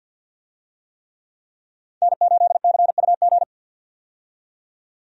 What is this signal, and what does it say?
Morse code, heard as a keyed tone, about 50 wpm, 700 Hz. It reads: N8CUG
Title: USA Call Signs - 5 Characters - 50wpm